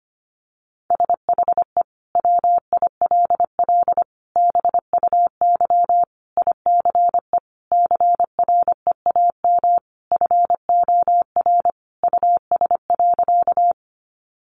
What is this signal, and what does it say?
Morse code, heard as a keyed tone, about 25 wpm, 700 Hz. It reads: SHE WILL BUY ICE CREAM FOR US.
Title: Sentences from Top 1500 Words 25wpm